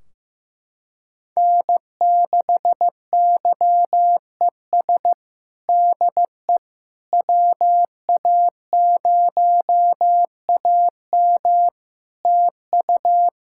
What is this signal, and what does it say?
Morse code, heard as a keyed tone, about 15 wpm, 700 Hz. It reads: N6YES DE WA0AM TU
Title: Practice QSO - Exchange 4 - Ending - 15wpm